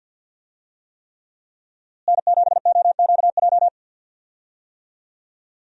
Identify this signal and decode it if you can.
Morse code, heard as a keyed tone, about 52 wpm, 700 Hz. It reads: N7OQJ
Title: USA Call Signs - 5 Characters - Repeated 3x - 50wpm